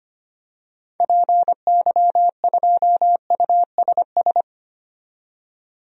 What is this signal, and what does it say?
Morse code, heard as a keyed tone, about 25 wpm, 700 Hz. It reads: PY2USS
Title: DXCC-100 Call Signs - 6 Characters - 25wpm